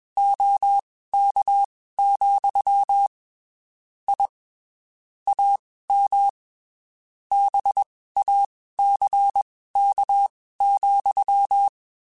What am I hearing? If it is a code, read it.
Morse code, heard as a keyed tone, about 21 wpm, spaced as for 15 wpm, 800 Hz. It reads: OK, I AM BACK,